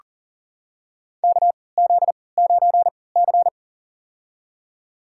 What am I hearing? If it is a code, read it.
Morse code, heard as a keyed tone, about 40 wpm, spaced as for 23 wpm, 700 Hz. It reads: KZ9C
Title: USA Call Signs - 4 Characters - 3x Character Spacing - 40wpm